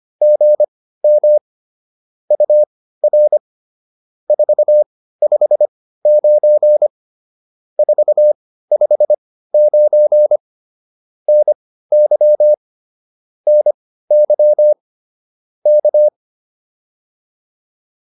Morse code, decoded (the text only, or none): GM UR 459 459 NY NY K